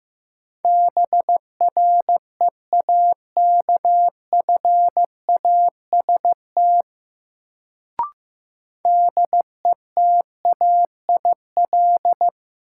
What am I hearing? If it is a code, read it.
Morse code, heard as a keyed tone, about 15 wpm, 700 Hz. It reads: BREAKFAST DETAIL
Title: Sentences from Top 1500 Words - New Word Drill Alt - 15wpm